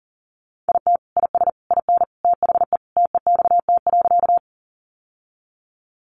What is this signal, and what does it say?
Morse code, heard as a keyed tone, about 40 wpm, 700 Hz. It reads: IT IS IN THE TEXT.
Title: Sentences from Top 1200 Words 40wpm